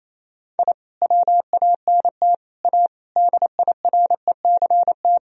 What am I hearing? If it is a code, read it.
Morse code, heard as a keyed tone, about 28 wpm, 700 Hz. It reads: I WANT A DIRECT
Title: Sentences from Top 1000 Words - Review Part 2 - 28wpm